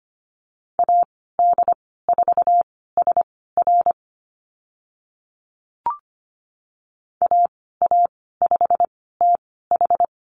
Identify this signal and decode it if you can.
Morse code, heard as a keyed tone, about 25 wpm, spaced as for 16 wpm, 690 Hz. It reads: AD4SR AA5TH
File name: Lesson 15g - Number 5 - Call Signs 25wpm